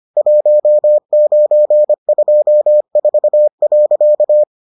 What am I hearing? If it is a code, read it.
Morse code, heard as a keyed tone, about 25 wpm, 600 Hz. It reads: 1924.